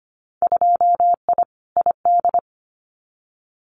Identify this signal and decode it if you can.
Morse code, heard as a keyed tone, about 25 wpm, 700 Hz. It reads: 2I ID
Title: Winter Field Day Exchanges Encore 25wpm